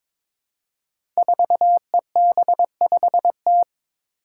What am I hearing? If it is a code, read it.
Morse code, heard as a keyed tone, about 22 wpm, 700 Hz. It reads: 4EB5T
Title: Five Challenging Letter-Number Combinations 22wpm